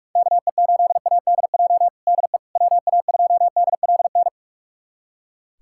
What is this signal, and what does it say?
Morse code, heard as a keyed tone, about 45 wpm, 700 Hz. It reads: KE8ADJ DE WA2DLN
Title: USA Callsign DE Callsign - 6 Characters - 45wpm